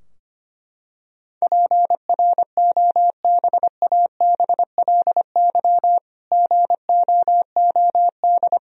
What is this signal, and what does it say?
Morse code, heard as a keyed tone, about 25 wpm, 700 Hz. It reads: PROBABLY GOOD
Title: Sentences from Top 1000 Words - Review Part 4 - 25wpm